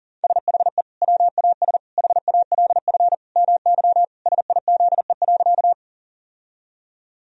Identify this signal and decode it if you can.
Morse code, heard as a keyed tone, about 40 wpm, 700 Hz. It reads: SHE WAS HALF MY SIZE.
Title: Sentences from Top 1000 Words - Review Part 2 - 40wpm